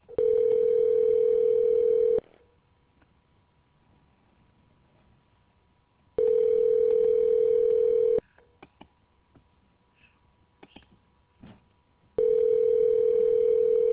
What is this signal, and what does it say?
Ringback tone (North America), 440 + 480 Hz.